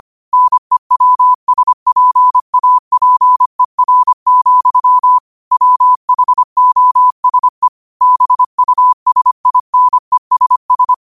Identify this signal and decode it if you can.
Morse code, heard as a keyed tone, about 25 wpm, 1000 Hz. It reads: NEWSPAPER, WHOSE BUSINESS
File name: qotdfiles / 25wpm